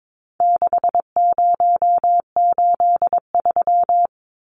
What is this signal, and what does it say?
Morse code, heard as a keyed tone, about 22 wpm, 700 Hz. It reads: 6083
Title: Numbers - 4-Digit 22wpm